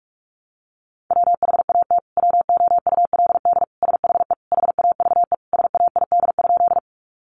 Morse code, heard as a keyed tone, about 44 wpm, 700 Hz. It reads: WHAT WOULD SHE HAVE SAID?
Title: Sentences from Top 100 Words 45wpm